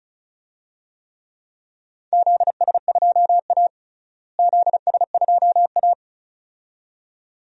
Morse code, heard as a keyed tone, about 34 wpm, 700 Hz. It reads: ZS2A ZS2A
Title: DXCC-100 Call Signs - 4 Characters - Repeated 2x - 35wpm